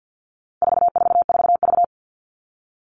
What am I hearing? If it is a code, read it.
Morse code, heard as a keyed tone, about 50 wpm, 700 Hz. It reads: 444V
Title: All Cut Numbers - 4-Digit 50wpm